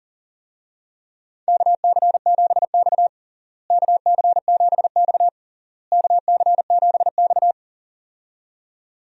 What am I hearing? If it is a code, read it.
Morse code, heard as a keyed tone, about 40 wpm, 700 Hz. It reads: KC7X KC7X KC7X
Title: USA Call Signs - 4 Characters - Repeated 3x - 40wpm